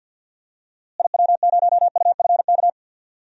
Morse code, heard as a keyed tone, about 51 wpm, 700 Hz. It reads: IW0UFK
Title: DXCC-100 Call Signs - 6 Characters - Repeated 2x - 50wpm